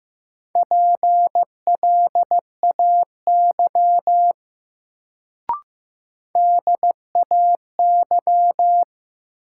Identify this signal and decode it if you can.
Morse code, heard as a keyed tone, about 15 wpm, 700 Hz. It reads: PLAY DAY
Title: Sentences from Top 200 Words - New Word Drill - 15wpm